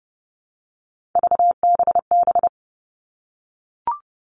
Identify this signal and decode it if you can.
Morse code, heard as a keyed tone, about 32 wpm, 700 Hz. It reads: VBB
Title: Three Challenging Letter-Number Combinations 30wpm